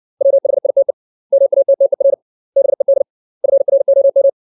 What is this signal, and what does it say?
Morse code, heard as a keyed tone, about 65 wpm, 550 Hz. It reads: WHITE MATTER BED FROM
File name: 632 words / 632 Words 60WPM